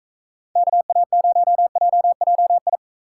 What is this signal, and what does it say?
Morse code, heard as a keyed tone, about 40 wpm, 700 Hz. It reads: KA0JJI